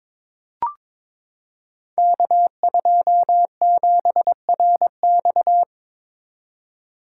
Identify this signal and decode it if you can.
Morse code, heard as a keyed tone, about 22 wpm, 700 Hz. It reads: K27RX
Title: Code Groups - 5 Letter:Numbers - 22wpm